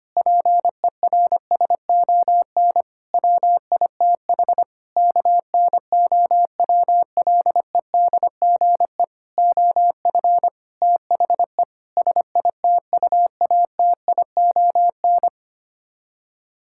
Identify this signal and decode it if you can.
Morse code, heard as a keyed tone, about 25 wpm, 700 Hz. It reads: PERSON WITH KNOWLEDGE OF THE SITUATION
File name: Sets of 9 Words - 25wpm